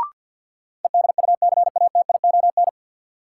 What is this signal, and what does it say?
Morse code, heard as a keyed tone, about 53 wpm, 700 Hz. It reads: EDUCATION
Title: Sentences from Top 1000 Words - Review Part 6 - New Word Drill - 50wpm